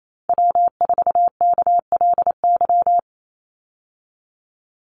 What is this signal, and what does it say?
Morse code, heard as a keyed tone, about 28 wpm, 700 Hz. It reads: W4KLY
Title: Call Signs Repeated 2x - 28wpm